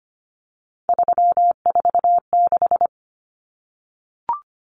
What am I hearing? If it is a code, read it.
Morse code, heard as a keyed tone, about 25 wpm, 700 Hz. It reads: 346
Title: Common Cut Numbers - 3-Digit 25wpm